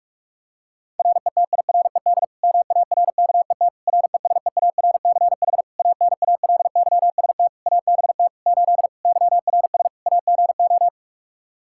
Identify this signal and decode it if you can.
Morse code, heard as a keyed tone, about 46 wpm, 700 Hz. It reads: RETIRED MARKET RESEARCH ANALYST ABT 8 YRS AGO
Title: Practice QSO - Exchange 3 - More Info - 45wpm